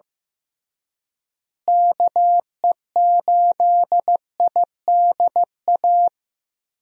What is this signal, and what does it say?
Morse code, heard as a keyed tone, about 15 wpm, 700 Hz. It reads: KE8IDA
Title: USA Call Signs - 6 Characters - Repeated 3x - 15wpm